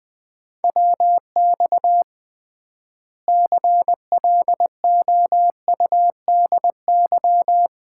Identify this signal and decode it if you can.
Morse code, heard as a keyed tone, about 21 wpm, 700 Hz. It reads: WX CLOUDY
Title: Common QSO Phrases - 3x Word-Spacing - 20wpm